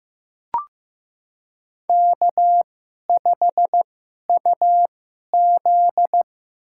Morse code, heard as a keyed tone, about 15 wpm, spaced as for 11 wpm, 700 Hz. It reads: K5UZ
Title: USA Call Signs - 4 Characters - 2x Character Spacing - 15wpm